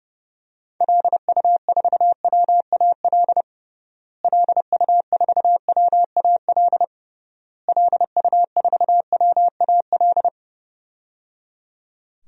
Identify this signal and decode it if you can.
Morse code, heard as a keyed tone, about 30 wpm, 700 Hz. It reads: LU4WAL LU4WAL LU4WAL
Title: DXCC-100 Call Signs - 6 Characters - Repeated 3x - 30wpm